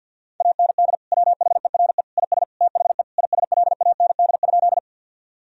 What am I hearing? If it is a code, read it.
Morse code, heard as a keyed tone, about 50 wpm, 700 Hz. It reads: AND WHERE IS THE ISLAND?
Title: Sentences from Top 1000 Words - Review Part 3 - 50wpm